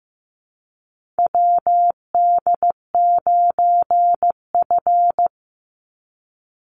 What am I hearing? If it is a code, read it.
Morse code, heard as a keyed tone, about 15 wpm, 700 Hz. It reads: WD9F
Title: NA SOTA Chaser Callsigns 15wpm